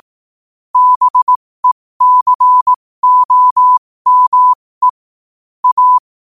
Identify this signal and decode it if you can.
Morse code, heard as a keyed tone, about 18 wpm, spaced as for 15 wpm, 1000 Hz. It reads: BECOME A
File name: qotdfiles / 15wpm